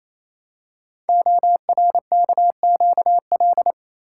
Morse code, heard as a keyed tone, about 27 wpm, 700 Hz. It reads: ORKQL
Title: Code Groups - 5 Letters - 28wpm